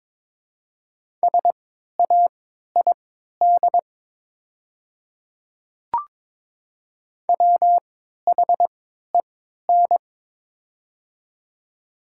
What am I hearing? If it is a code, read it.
Morse code, heard as a keyed tone, about 22 wpm, spaced as for 12 wpm, 700 Hz. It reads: SAID WHEN
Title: 3x Character Spacing - Top 100 Words - 22wpm